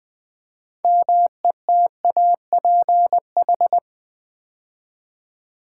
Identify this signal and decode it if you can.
Morse code, heard as a keyed tone, about 20 wpm, 700 Hz. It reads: METAPH
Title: Build A Word - 2 Letters At A Time - 8-Letter Words - 20wpm